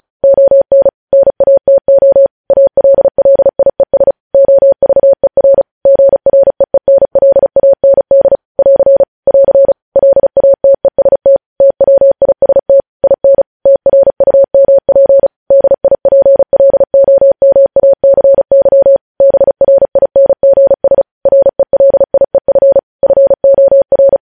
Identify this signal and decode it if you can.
Morse code, heard as a keyed tone, about 36 wpm, 550 Hz. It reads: ON NATO ALLIES OVER GREENLAND <AR> <AR> LATEST TWIST IN TRUMP DIPLOMACY BRINGS RELIEF FOR
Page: News Headlines in Morse Code at 35 WPM